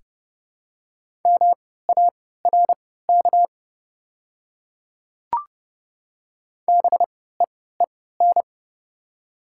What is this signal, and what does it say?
Morse code, heard as a keyed tone, about 28 wpm, spaced as for 17 wpm, 690 Hz. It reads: MARK BEEN
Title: 3x Character Spacing - Top 300 Words - 30wpm